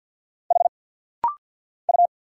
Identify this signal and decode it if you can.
Morse code, heard as a keyed tone, about 50 wpm, 700 Hz. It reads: H U
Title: Single Letter-Number Alt - Rapid-Fire 50wpm